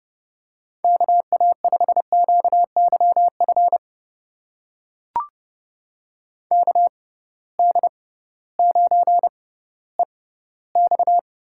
Morse code, heard as a keyed tone, about 30 wpm, spaced as for 16 wpm, 700 Hz. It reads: KA5QYF KD9EX
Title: USA Call Signs - 6 Characters - 6x Character Spacing - 30wpm